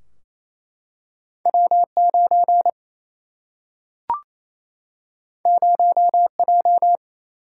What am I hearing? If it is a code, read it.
Morse code, heard as a keyed tone, about 28 wpm, 700 Hz. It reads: W9 0J
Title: Single-Letter Single-Number 28wpm